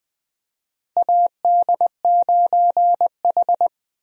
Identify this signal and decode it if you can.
Morse code, heard as a keyed tone, about 20 wpm, 700 Hz. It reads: AD9H
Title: USA Call Signs - 4 Characters - Repeated 3x - 20wpm